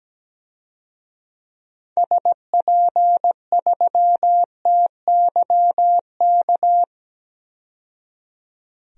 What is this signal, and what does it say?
Morse code, heard as a keyed tone, about 17 wpm, 700 Hz. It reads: SP3TYK
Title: DXCC-100 Call Signs - 6 Characters - 17wpm